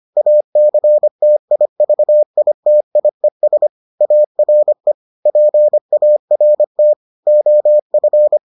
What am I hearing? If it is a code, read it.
Morse code, heard as a keyed tone, about 25 wpm, 600 Hz. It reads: ACTIVITIES ARE PART OF